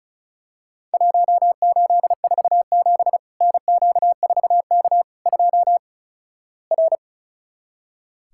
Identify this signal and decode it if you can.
Morse code, heard as a keyed tone, about 36 wpm, 700 Hz. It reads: 1847 NQ4K 2 R
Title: Worked All Europe DX Contest - QTC Exchange - 35wpm